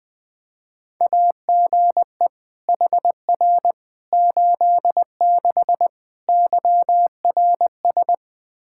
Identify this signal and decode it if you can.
Morse code, heard as a keyed tone, about 20 wpm, 700 Hz. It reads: AGE HR 86 YRS
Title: Practice QSO - Exchange 3 - More Info - 20wpm